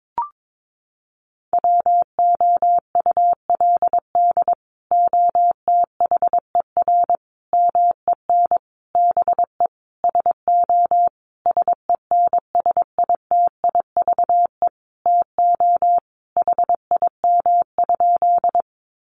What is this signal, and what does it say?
Morse code, heard as a keyed tone, about 22 wpm, 700 Hz. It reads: WOULD OTHER MEN BE SO SENSITIVE TO HIM?
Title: Sentences from Top 1400 Words 22wpm